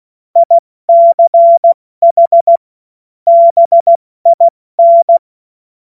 Morse code, heard as a keyed tone, about 16 wpm, 680 Hz. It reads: ICH BIN
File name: Morsekurs / Witze / Witz 13. September 2022 70 F80